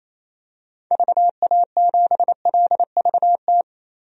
Morse code, heard as a keyed tone, about 28 wpm, 700 Hz. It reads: VA7LVT
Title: DXCC-100 Call Signs - 6 Characters - Repeated 3x - 28wpm